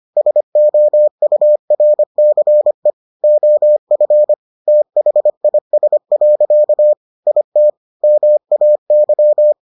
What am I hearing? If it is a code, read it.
Morse code, heard as a keyed tone, about 25 wpm, 600 Hz. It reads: SOURCE OF THIS. IT MAY